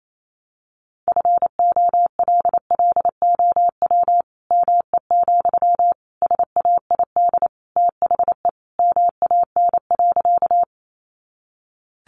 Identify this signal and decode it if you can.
Morse code, heard as a keyed tone, about 28 wpm, 700 Hz. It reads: FOLLOW ME, SAID THE MAN.
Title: Sentences from Top 200 Words 28wpm